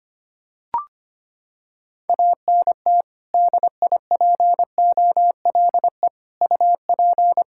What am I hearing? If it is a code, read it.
Morse code, heard as a keyed tone, about 25 wpm, 700 Hz. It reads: ANT DIPOLE UP
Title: Practice QSO - Exchange 2 - Info - 25wpm